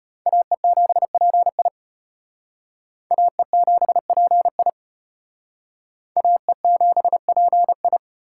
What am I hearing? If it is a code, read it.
Morse code, heard as a keyed tone, about 31 wpm, 700 Hz. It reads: AE7PI AE7PI AE7PI
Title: Speed Racing - USA Call Signs 25wpm